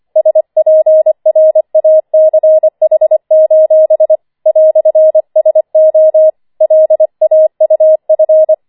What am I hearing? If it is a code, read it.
Morse code, heard as a keyed tone, about 25 wpm, 630 Hz. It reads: SPRACH: "SO LAUF
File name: 25wpm